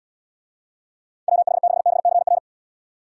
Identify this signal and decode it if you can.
Morse code, heard as a keyed tone, about 71 wpm, 700 Hz. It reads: WH6BCR